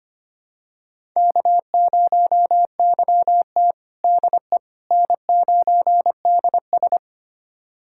Morse code, heard as a keyed tone, about 25 wpm, 700 Hz. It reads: K0YT DE N9DS
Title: USA Callsign DE Callsign - 4 Characters - 25wpm